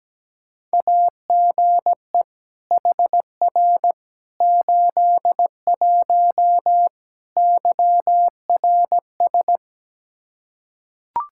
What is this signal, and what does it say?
Morse code, heard as a keyed tone, about 17 wpm, 700 Hz. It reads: AGE HR 81 YRS
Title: Practice QSO - Exchange 3 - More Info - 17wpm